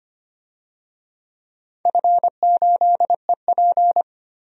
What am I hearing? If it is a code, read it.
Morse code, heard as a keyed tone, about 25 wpm, 700 Hz. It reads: F8EP
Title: DXCC-100 Call Signs - 4 Characters - 25wpm